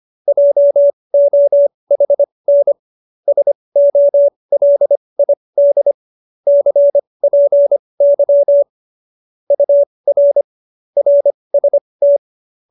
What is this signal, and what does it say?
Morse code, heard as a keyed tone, about 25 wpm, spaced as for 20 wpm, 570 Hz. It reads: JOHN SOLID CPY UR RST
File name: qso216b 20